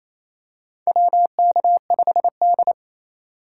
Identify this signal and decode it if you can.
Morse code, heard as a keyed tone, about 28 wpm, 700 Hz. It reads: WK5D
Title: USA Call Signs - 3x Character Spacing - 28wpm